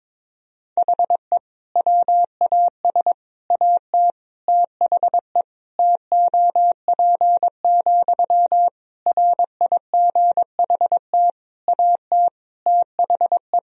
Morse code, heard as a keyed tone, about 22 wpm, 700 Hz. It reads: HE WAS AT THE TOP, RIGHT AT THE
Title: Sentences from Top 1000 Words - Review Part 2 - 22wpm